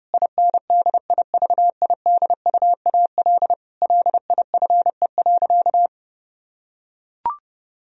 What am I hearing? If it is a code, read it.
Morse code, heard as a keyed tone, about 30 wpm, 700 Hz. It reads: INDIVIDUAL LIFE.
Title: Sentences from Top 1000 Words - Review Part 4 - 30wpm